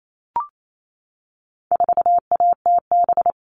Morse code, heard as a keyed tone, about 28 wpm, 700 Hz. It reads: 4ATB
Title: All Cut Numbers - 4-Digit 28wpm